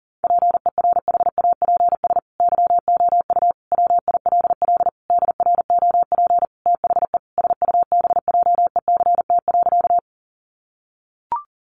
Morse code, heard as a keyed tone, about 40 wpm, 700 Hz. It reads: PERHAPS YOU WILL DROP THE SUBJECT.
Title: Sentences from Top 1400 Words 40wpm